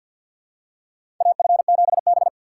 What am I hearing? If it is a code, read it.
Morse code, heard as a keyed tone, about 47 wpm, 700 Hz. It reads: AF7B
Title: USA Call Signs - 4 Characters - 2x Character Spacing - 50wpm